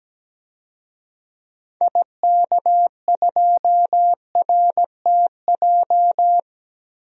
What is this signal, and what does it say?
Morse code, heard as a keyed tone, about 17 wpm, 700 Hz. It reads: IK2RTJ
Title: DXCC-100 Call Signs - 6 Characters - Repeated 2x - 17wpm